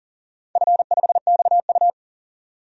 Morse code, heard as a keyed tone, about 40 wpm, 700 Hz. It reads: F5XU
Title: DXCC-100 Call Signs - 4 Characters - 40wpm